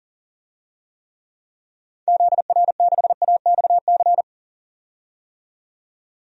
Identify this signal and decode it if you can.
Morse code, heard as a keyed tone, about 40 wpm, 700 Hz. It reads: ZR6AXC
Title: DXCC-100 Call Signs - 6 Characters - 40wpm